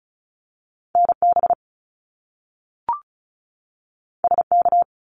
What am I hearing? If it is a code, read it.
Morse code, heard as a keyed tone, about 36 wpm, 700 Hz. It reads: NB SK
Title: Canadian Provinces and Territories 35wpm